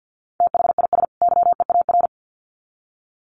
Morse code, heard as a keyed tone, about 50 wpm, 700 Hz. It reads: THIS YEAR
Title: Sets of 2 Words 50wpm